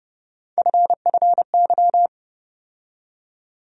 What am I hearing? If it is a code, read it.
Morse code, heard as a keyed tone, about 31 wpm, 700 Hz. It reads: FFY
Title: USA Call Sign Postfixes - 3 Letters 30wpm